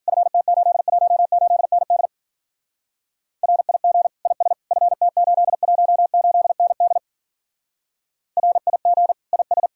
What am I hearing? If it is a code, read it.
Morse code, heard as a keyed tone, about 50 wpm, 700 Hz. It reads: FT818ND RIG IS FT818ND RIG IS
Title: Speed-Racing Common QSO Phrases - 40wpm